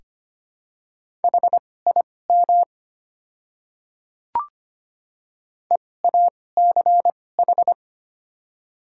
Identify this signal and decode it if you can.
Morse code, heard as a keyed tone, about 25 wpm, spaced as for 18 wpm, 700 Hz. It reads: HIM EACH
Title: 2x Character Spacing - Top 300 Words - 25wpm